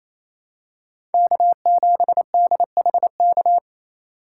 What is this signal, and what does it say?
Morse code, heard as a keyed tone, about 28 wpm, 700 Hz. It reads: K7DHK
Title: USA Call Signs - 5 Characters - 2x Character Spacing - 28wpm